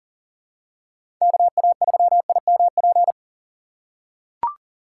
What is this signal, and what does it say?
Morse code, heard as a keyed tone, about 40 wpm, 700 Hz. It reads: KA3IMP
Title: USA Call Signs - 6x Character Spacing - 40wpm